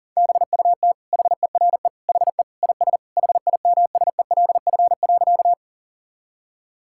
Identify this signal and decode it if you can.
Morse code, heard as a keyed tone, about 40 wpm, 700 Hz. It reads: BUT HERE HE IS HIMSELF.
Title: Sentences from Top 1600 Words 40wpm